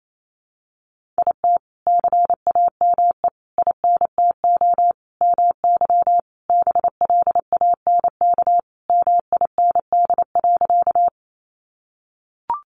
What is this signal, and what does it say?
Morse code, heard as a keyed tone, about 28 wpm, 700 Hz. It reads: IT CAME INTO MY BLANK MIND.
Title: Sentences from Top 1500 Words 28wpm